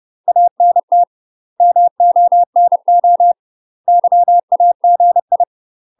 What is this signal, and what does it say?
Morse code, heard as a keyed tone, about 30 wpm, 700 Hz. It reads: ANT MONO YAGI
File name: UK MASTER QSOs TRAINING / 30WPM / UK MASTER QSOs 0040